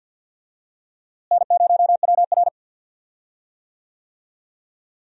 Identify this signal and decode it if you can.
Morse code, heard as a keyed tone, about 52 wpm, 700 Hz. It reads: N0WR
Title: USA Call Signs - 4 Characters - Repeated 2x - 50wpm